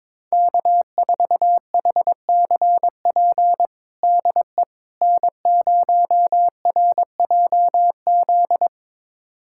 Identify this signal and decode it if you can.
Morse code, heard as a keyed tone, about 22 wpm, 700 Hz. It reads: K4HCP DE N0RJZ
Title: USA Callsign DE Callsign - 5 Characters - 22wpm